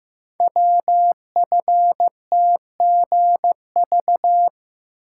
Morse code, heard as a keyed tone, about 15 wpm, 700 Hz. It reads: WFTGV
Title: Code Groups - 5 Letters - 15wpm